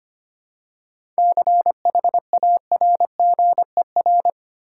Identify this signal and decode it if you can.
Morse code, heard as a keyed tone, about 25 wpm, 700 Hz. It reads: CHARGER
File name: practice / 25wpm / Easy Words with Hints - 25wpm